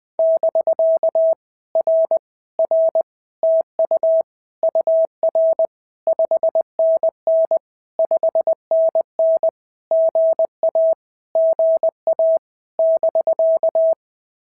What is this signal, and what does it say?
Morse code, heard as a keyed tone, about 20 wpm, 650 Hz. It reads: <BK> R R TU UR 5NN 5NN GA GA <BK>
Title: POTA QSO Practice - 20wpm